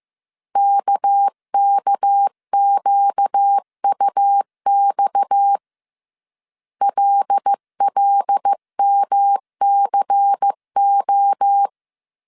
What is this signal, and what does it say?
Morse code, heard as a keyed tone, about 15 wpm, 800 Hz. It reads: KKQUX LLMCO
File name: CW / DET / 1 6 1500C S C 14wpm 2951